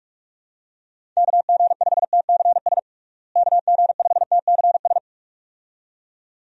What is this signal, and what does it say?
Morse code, heard as a keyed tone, about 46 wpm, 700 Hz. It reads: KG5TCS KG5TCS
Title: USA Call Signs - 6 Characters - Repeated 2x - 45wpm